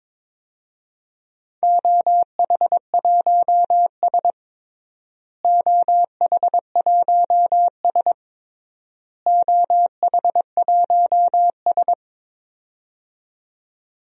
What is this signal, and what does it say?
Morse code, heard as a keyed tone, about 22 wpm, 700 Hz. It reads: OH1S OH1S OH1S
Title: DXCC-100 Call Signs - 4 Characters - Repeated 3x - 22wpm